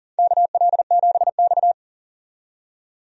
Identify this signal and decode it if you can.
Morse code, heard as a keyed tone, about 40 wpm, 700 Hz. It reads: KL7X
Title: DXCC-100 Call Signs - 4 Characters - 40wpm